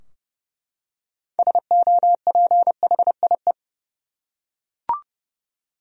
Morse code, heard as a keyed tone, about 30 wpm, 700 Hz. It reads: SOPHIE
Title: USA Common Names - 30wpm